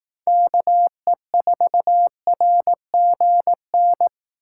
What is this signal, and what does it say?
Morse code, heard as a keyed tone, about 18 wpm, 700 Hz. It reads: KE4RGN
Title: Speed Racing - USA 2x3 Call Signs 15wpm